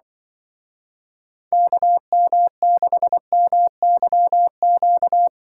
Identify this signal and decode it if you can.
Morse code, heard as a keyed tone, about 25 wpm, 700 Hz. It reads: KM6MYQ